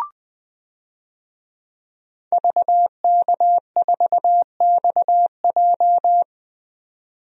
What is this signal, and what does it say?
Morse code, heard as a keyed tone, about 20 wpm, 700 Hz. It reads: VK4XJ
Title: DXCC-100 Call Signs - 5 Characters - 20wpm